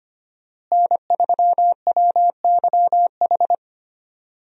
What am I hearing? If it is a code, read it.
Morse code, heard as a keyed tone, about 25 wpm, 700 Hz. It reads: N3WYH
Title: USA Call Signs - 5 Characters - Repeated 3x - 25wpm